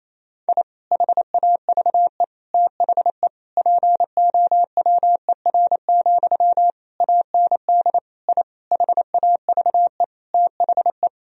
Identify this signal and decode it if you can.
Morse code, heard as a keyed tone, about 27 wpm, 700 Hz. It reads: I HAVE THE POWER, AND I HAVE THE
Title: Sentences from Top 1000 Words - Review Part 2 - 28wpm